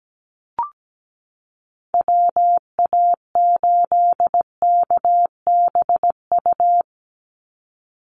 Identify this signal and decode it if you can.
Morse code, heard as a keyed tone, about 17 wpm, 700 Hz. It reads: WA8KBU
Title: Call Signs Repeated 3x - 17wpm